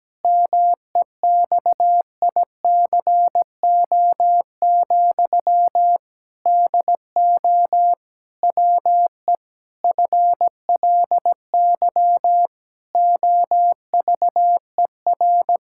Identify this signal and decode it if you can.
Morse code, heard as a keyed tone, about 17 wpm, 700 Hz. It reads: MEXICO, DO WE FLY OVER
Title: Pangram Challenge v2 - 17wpm